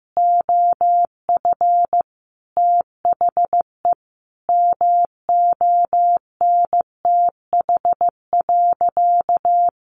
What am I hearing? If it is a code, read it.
Morse code, heard as a keyed tone, about 15 wpm, 700 Hz. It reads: OF THE MONTH.
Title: Sentences from Top 800 Words 15wpm